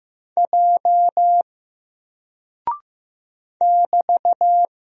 Morse code, heard as a keyed tone, about 15 wpm, 700 Hz. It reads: J <BT>
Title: Single Character (Letter:Number:Punctuation:Prosign) Alt 15wpm